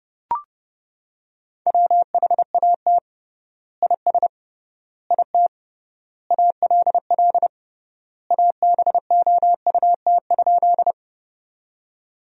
Morse code, heard as a keyed tone, about 30 wpm, 700 Hz. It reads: WHAT IS IT ALL ABOUT?
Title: 3x Word-Spacing - Sentences from Top 100 Words - 30wpm